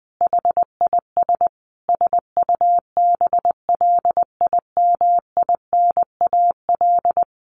Morse code, heard as a keyed tone, about 20 wpm, 700 Hz. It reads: HIS SUBLIMINAL
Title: A Sound Of Thunder 20wpm